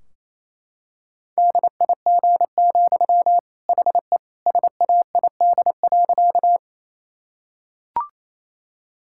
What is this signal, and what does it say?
Morse code, heard as a keyed tone, about 28 wpm, 700 Hz. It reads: DIG, HE SAID.
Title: Sentences from Top 1600 Words 28wpm